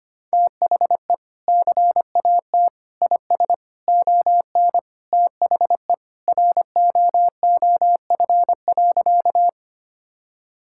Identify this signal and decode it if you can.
Morse code, heard as a keyed tone, about 25 wpm, 700 Hz. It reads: THE CAT IS ON THE ROOF.
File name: practice / 25wpm / Sentences with 3-Word Phrases - 25wpm